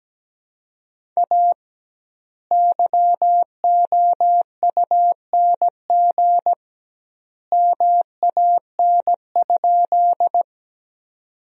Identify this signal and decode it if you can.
Morse code, heard as a keyed tone, about 17 wpm, 700 Hz. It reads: A YOUNG MAN?
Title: 2x Word-Spacing - Sentences from Top 400 Words - 17wpm